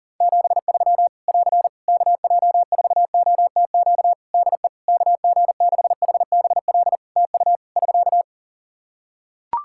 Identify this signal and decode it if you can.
Morse code, heard as a keyed tone, about 37 wpm, 700 Hz. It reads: 73 <AR> KJ4OTQ DE KG6HBL TU <SK>
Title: Practice QSO - Exchange 4 - Ending - 40wpm